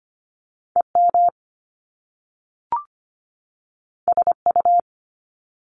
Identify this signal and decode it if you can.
Morse code, heard as a keyed tone, about 25 wpm, 700 Hz. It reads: EM SU
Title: Top Two Letter Combinations in Words 25wpm